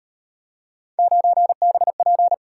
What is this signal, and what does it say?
Morse code, heard as a keyed tone, about 35 wpm, 700 Hz. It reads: 9BP